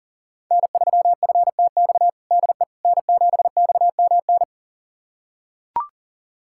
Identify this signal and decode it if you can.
Morse code, heard as a keyed tone, about 40 wpm, 700 Hz. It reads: N3FTX DE N7XMN
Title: USA Callsign DE Callsign - 5 Characters - 40wpm